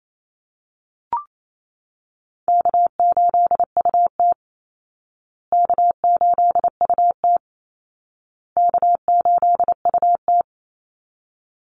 Morse code, heard as a keyed tone, about 28 wpm, 700 Hz. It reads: K8UT K8UT K8UT
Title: Easy Call Signs Repeated 3x - 28wpm